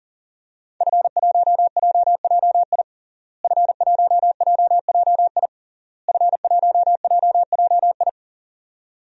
Morse code, heard as a keyed tone, about 40 wpm, 700 Hz. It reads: F1JJI F1JJI F1JJI
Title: DXCC-100 Call Signs - 5 Characters - Repeated 3x - 40wpm